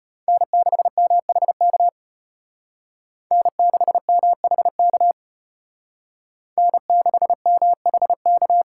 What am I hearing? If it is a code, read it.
Morse code, heard as a keyed tone, about 32 wpm, 700 Hz. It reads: N6MHK N6MHK N6MHK